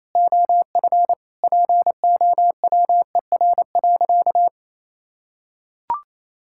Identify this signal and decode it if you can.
Morse code, heard as a keyed tone, about 28 wpm, 700 Hz. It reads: OF POWER.
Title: Sentences from Top 1000 Words - Review Part 6 - 28wpm